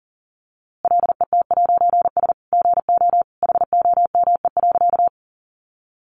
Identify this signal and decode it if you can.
Morse code, heard as a keyed tone, about 40 wpm, 700 Hz. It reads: LET'S GO HOME.
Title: A Sound Of Thunder 40wpm